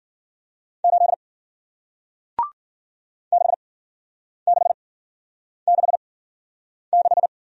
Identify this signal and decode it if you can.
Morse code, heard as a keyed tone, about 56 wpm, spaced as for 9 wpm, 700 Hz. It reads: 8 6666